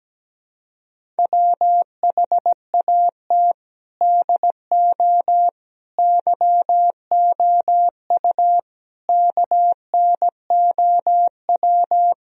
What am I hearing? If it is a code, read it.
Morse code, heard as a keyed tone, about 17 wpm, 700 Hz. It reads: WHAT DO YOU KNOW
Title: Sentences from Top 1000 Words - Review Part 4 - 17wpm